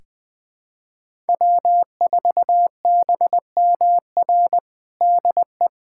Morse code, heard as a keyed tone, about 20 wpm, 700 Hz. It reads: W4BMR DE
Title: USA Callsign DE Callsign - 5 Characters - 20wpm